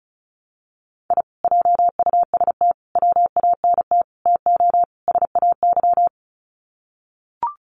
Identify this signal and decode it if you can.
Morse code, heard as a keyed tone, about 34 wpm, 700 Hz. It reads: I JUST WANT TO SAY
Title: Sets of 5 Words Encore 35wpm